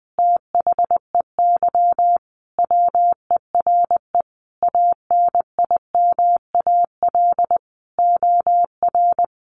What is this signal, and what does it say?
Morse code, heard as a keyed tone, about 20 wpm, 700 Hz. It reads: THEY WERE ANIMAL OR
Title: Sentences from Top 1300 Words 20wpm